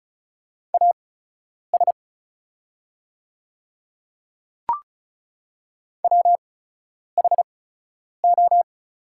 Morse code, heard as a keyed tone, about 36 wpm, spaced as for 10 wpm, 700 Hz. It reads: AS WHO
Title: 8x Character Spacing - Top 100 Words - 35wpm